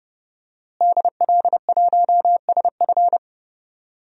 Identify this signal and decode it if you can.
Morse code, heard as a keyed tone, about 30 wpm, 700 Hz. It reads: DL1SF
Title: DXCC-100 Call Signs - 5 Characters - Repeated 2x - 30wpm